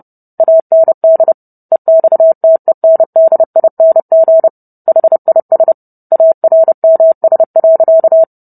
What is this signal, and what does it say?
Morse code, heard as a keyed tone, about 30 wpm, 650 Hz. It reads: AND EXTENDING HIS ARMS.